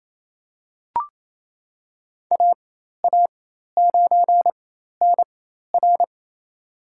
Morse code, heard as a keyed tone, about 28 wpm, spaced as for 13 wpm, 700 Hz. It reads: AA9NR
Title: USA Call Signs - 4x Character Spacing - 28wpm